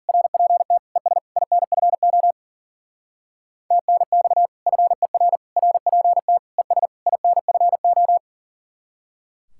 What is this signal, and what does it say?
Morse code, heard as a keyed tone, about 50 wpm, 700 Hz. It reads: RPT ES INFO TNX FER RPT ES INFO
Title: Speed-Racing Common QSO Phrases Encore - 40wpm